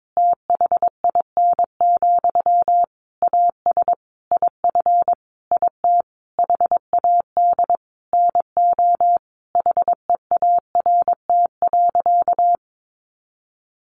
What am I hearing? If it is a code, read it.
Morse code, heard as a keyed tone, about 22 wpm, 700 Hz. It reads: THIN, AS IF IT HAD NO HEART.
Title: Sentences from Top 1200 Words 22wpm